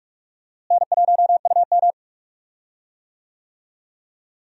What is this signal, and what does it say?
Morse code, heard as a keyed tone, about 46 wpm, 700 Hz. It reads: N1UM
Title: USA Call Signs - 4 Characters - Repeated 2x - 45wpm